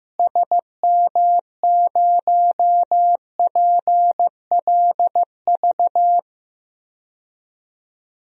Morse code, heard as a keyed tone, about 15 wpm, 700 Hz. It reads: SM0PLV
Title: DXCC-100 Call Signs - 6 Characters - Repeated 2x - 15wpm